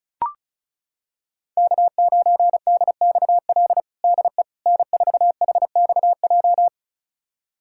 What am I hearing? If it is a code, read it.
Morse code, heard as a keyed tone, about 36 wpm, 700 Hz. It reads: K9DXL DE N4HXJ
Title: USA Callsign DE Callsign - 5 Characters - 35wpm